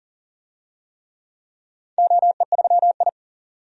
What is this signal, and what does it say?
Morse code, heard as a keyed tone, about 38 wpm, 700 Hz. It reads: OE3I
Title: DXCC-100 Call Signs - 4 Characters - 40wpm